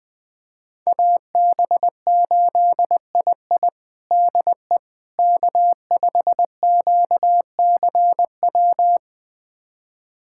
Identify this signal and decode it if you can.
Morse code, heard as a keyed tone, about 20 wpm, 700 Hz. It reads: AB8II DE K5QCW
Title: USA Callsign DE Callsign - 5 Characters - 20wpm